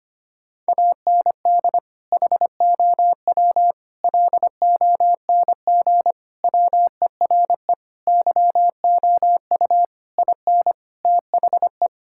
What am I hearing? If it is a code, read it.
Morse code, heard as a keyed tone, about 25 wpm, 700 Hz. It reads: AND HOW LONG WERE YOU IN THE
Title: Sentences from Top 1600 Words 25wpm